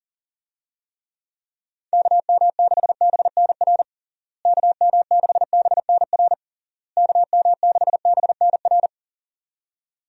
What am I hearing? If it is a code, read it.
Morse code, heard as a keyed tone, about 40 wpm, 700 Hz. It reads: KM6BNR KM6BNR KM6BNR
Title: USA Call Signs - 6 Characters - Repeated 3x - 40wpm